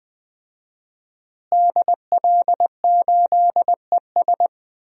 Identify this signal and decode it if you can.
Morse code, heard as a keyed tone, about 20 wpm, 700 Hz. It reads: DL8ES
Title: DXCC-100 Call Signs - 5 Characters - Repeated 3x - 20wpm